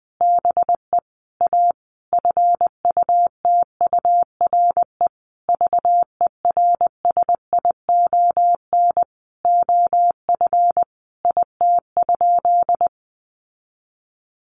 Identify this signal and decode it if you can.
Morse code, heard as a keyed tone, about 20 wpm, 700 Hz. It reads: BE A FUTURE VERSION OF IT?
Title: Sentences from Top 1400 Words 20wpm